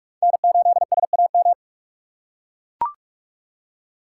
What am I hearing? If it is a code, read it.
Morse code, heard as a keyed tone, about 42 wpm, 700 Hz. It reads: N8SAM
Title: USA Call Signs - 5 Characters - 6x Character Spacing - 45wpm